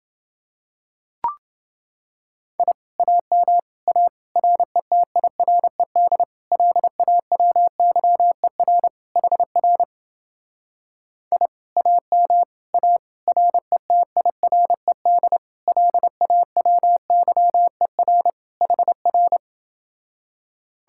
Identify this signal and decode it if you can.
Morse code, heard as a keyed tone, about 28 wpm, 700 Hz. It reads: I AM A RETIRED LAWYER HR I AM A RETIRED LAWYER HR